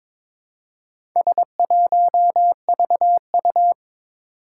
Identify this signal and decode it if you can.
Morse code, heard as a keyed tone, about 22 wpm, 700 Hz. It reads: S1VU
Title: Four Challenging Letter-Number Combinations 22wpm